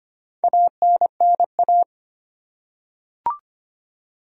Morse code, heard as a keyed tone, about 25 wpm, 700 Hz. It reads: ANNA
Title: USA Common Names - Max 5 Letters - 25wpm